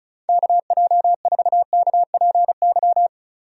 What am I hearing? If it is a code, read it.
Morse code, heard as a keyed tone, about 34 wpm, 700 Hz. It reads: KJ4KPY
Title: USA Call Signs - 6 Characters - Repeated 2x - 35wpm